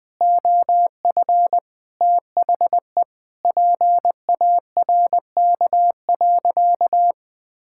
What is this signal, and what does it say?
Morse code, heard as a keyed tone, about 20 wpm, 700 Hz. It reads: OF THE PARK.
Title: Sentences with 5-Word Phrases - Spoken Beginning - 20wpm